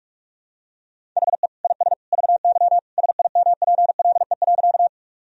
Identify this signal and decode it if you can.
Morse code, heard as a keyed tone, about 42 wpm, 700 Hz. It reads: HE IS VY SIMPLE.
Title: Sentences from Top 400 Words and Abbreviations 45wpm